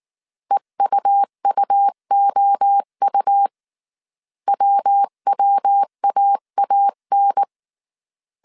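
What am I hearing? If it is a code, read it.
Morse code, heard as a keyed tone, about 19 wpm, 800 Hz. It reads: EUUOU WWAAN
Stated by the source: Morse code dictation at 18 wpm from the IW3SGT QRS Morse code training course, five-character groups of 300 characters.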